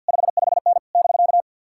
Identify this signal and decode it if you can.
Morse code, heard as a keyed tone, about 66 wpm, spaced as for 58 wpm, 700 Hz. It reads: 55N <BK>
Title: Speed-Racing Common QSO Phrases - 50wpm